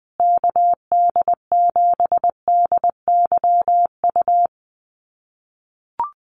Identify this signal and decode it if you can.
Morse code, heard as a keyed tone, about 20 wpm, 700 Hz. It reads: KD7DYU
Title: USA Call Signs Repeated 3x 20wpm